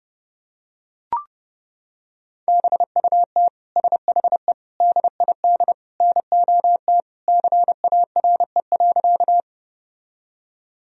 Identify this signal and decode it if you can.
Morse code, heard as a keyed tone, about 32 wpm, 700 Hz. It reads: BUT SHE DID NOT CARE.
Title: Sentences from Top 1000 Words - Review Part 1 - 30wpm